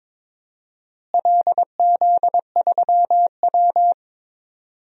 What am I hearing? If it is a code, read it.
Morse code, heard as a keyed tone, about 21 wpm, 700 Hz. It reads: LZ3W
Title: DXCC-100 Call Signs - 4 Characters - Repeated 3x - 22wpm